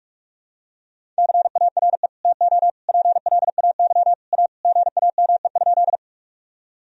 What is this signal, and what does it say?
Morse code, heard as a keyed tone, about 43 wpm, 700 Hz. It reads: CARE TO PLAY A GAME?
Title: Sentences from Top 1000 Words - Review Part 1 - 45wpm